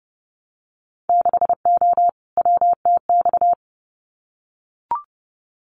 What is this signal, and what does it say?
Morse code, heard as a keyed tone, about 30 wpm, 700 Hz. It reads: 6O WTX
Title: Winter Field Day Exchanges Encore 30wpm